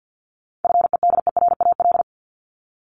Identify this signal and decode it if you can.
Morse code, heard as a keyed tone, about 48 wpm, 700 Hz. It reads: FEDERAL
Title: Top 751-1000 Words 50wpm